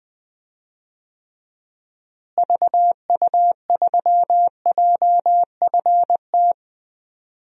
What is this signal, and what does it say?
Morse code, heard as a keyed tone, about 21 wpm, 700 Hz. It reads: VU3JFT
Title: DXCC-100 Call Signs - 6 Characters - Repeated 3x - 20wpm